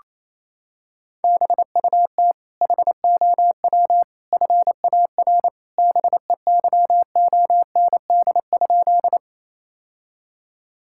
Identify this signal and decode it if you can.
Morse code, heard as a keyed tone, about 28 wpm, 700 Hz. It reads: BUT HOW FAR BEYOND?
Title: Sentences from Top 1000 Words - Review Part 6 - 28wpm